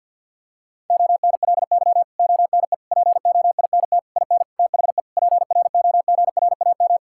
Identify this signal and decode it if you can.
Morse code, heard as a keyed tone, about 50 wpm, 700 Hz. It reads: ONLY ONE POINT IN THE PROGRAM
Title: Sentences from Top 1000 Words - Review Part 5 - 50wpm